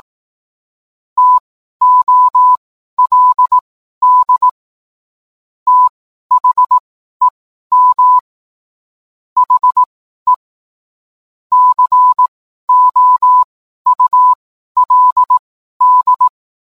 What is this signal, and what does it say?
Morse code, heard as a keyed tone, about 18 wpm, spaced as for 12 wpm, 1000 Hz. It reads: TOLD THEM HE COULD